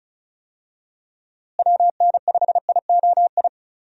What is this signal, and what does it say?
Morse code, heard as a keyed tone, about 36 wpm, 700 Hz. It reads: WN5IOI
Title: USA Call Signs - 6 Characters - Repeated 2x - 35wpm